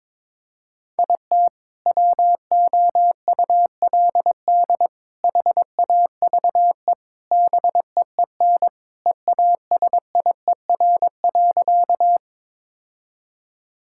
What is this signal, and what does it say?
Morse code, heard as a keyed tone, about 22 wpm, 700 Hz. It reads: IT WOULD HAVE BEEN EASIER.
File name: practice / 22wpm / Sentences with 4-Word Phrases - 22wpm